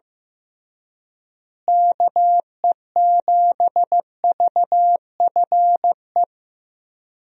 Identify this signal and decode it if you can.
Morse code, heard as a keyed tone, about 16 wpm, 700 Hz. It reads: KE7VFE
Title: USA Call Signs - 6 Characters - Repeated 3x - 15wpm